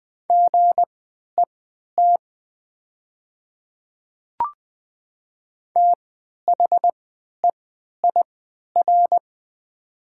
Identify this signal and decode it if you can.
Morse code, heard as a keyed tone, about 20 wpm, spaced as for 11 wpm, 700 Hz. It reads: GET THEIR
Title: 3x Character Spacing - Top 200 Words - 20wpm